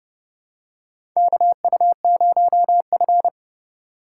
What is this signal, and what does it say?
Morse code, heard as a keyed tone, about 30 wpm, 700 Hz. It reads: KU0F
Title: USA Call Signs - 3x Character Spacing - 30wpm